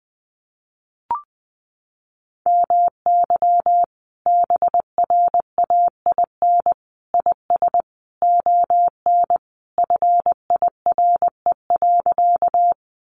Morse code, heard as a keyed tone, about 20 wpm, 700 Hz. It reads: MY BRAIN IS ON FIRE.
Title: Sentences from Top 1100 Words 20wpm